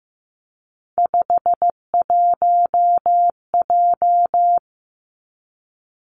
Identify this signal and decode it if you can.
Morse code, heard as a keyed tone, about 15 wpm, 700 Hz. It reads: 51J
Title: Three Challenging Letter-Number Combinations 15wpm